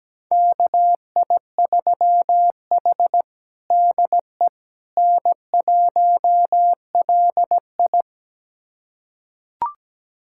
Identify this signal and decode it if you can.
Morse code, heard as a keyed tone, about 17 wpm, 700 Hz. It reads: KI3H DE N1LI
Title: USA Callsign DE Callsign - 4 Characters - 17wpm